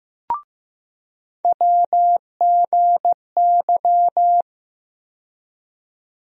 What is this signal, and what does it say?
Morse code, heard as a keyed tone, about 15 wpm, 700 Hz. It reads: WGY
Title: USA Call Sign Postfixes - 3 Letters 15wpm